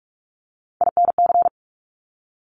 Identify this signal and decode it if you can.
Morse code, heard as a keyed tone, about 46 wpm, 690 Hz. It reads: INC
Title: Top Three Letter Combinations in Words 45wpm